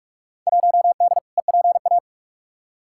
Morse code, heard as a keyed tone, about 46 wpm, 700 Hz. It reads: 1D EPA
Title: Field Day Exchanges - 45wpm